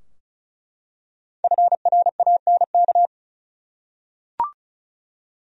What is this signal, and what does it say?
Morse code, heard as a keyed tone, about 34 wpm, 700 Hz. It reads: FRANK
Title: USA Common Names - 35wpm